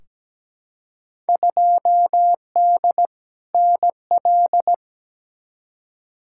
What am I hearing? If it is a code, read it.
Morse code, heard as a keyed tone, about 17 wpm, 700 Hz. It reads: 2D NL
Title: Field Day Exchanges Encore - 17wpm